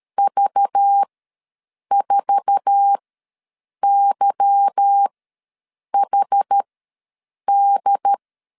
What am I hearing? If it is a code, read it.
Morse code, heard as a keyed tone, about 13 wpm, spaced as for 7 wpm, 800 Hz. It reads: V4YHD
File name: LEZIONI / 8 YJ45 / 8 YJ45 ws 12wpm 3146